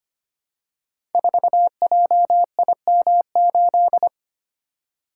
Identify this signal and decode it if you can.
Morse code, heard as a keyed tone, about 25 wpm, 700 Hz. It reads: 4JIM8
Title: Code Groups - 5 Letter:Numbers - 25wpm